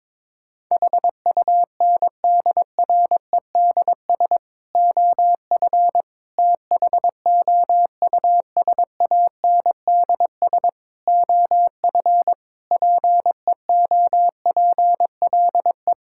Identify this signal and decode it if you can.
Morse code, heard as a keyed tone, about 22 wpm, 700 Hz. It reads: HUNDREDS OF THOUSANDS OF PEOPLE
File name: practice / 22wpm / Sentences with 4-Word Phrases - 22wpm